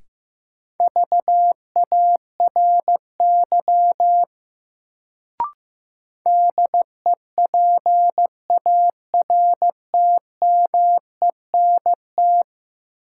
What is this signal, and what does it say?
Morse code, heard as a keyed tone, about 15 wpm, 700 Hz. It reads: VARY DEPARTMENT
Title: Sentences from Top 1200 Words - New Word Drill Alt - 15wpm